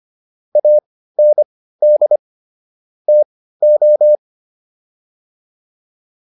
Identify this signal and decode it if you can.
Morse code, heard as a keyed tone, about 25 wpm, spaced as for 15 wpm, 600 Hz. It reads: AND TO